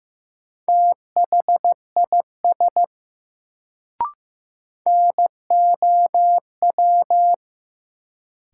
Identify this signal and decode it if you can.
Morse code, heard as a keyed tone, about 15 wpm, 700 Hz. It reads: THIS NOW
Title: Sets of 2 Words - New Word Drill - 15wpm